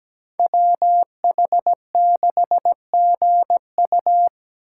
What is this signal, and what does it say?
Morse code, heard as a keyed tone, about 17 wpm, 700 Hz. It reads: WH6GU
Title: USA Call Signs - 3x Character Spacing - 17wpm